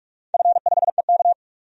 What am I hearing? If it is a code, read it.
Morse code, heard as a keyed tone, about 45 wpm, 700 Hz. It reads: F5EK
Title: DXCC-100 Call Signs - 4 Characters - 45wpm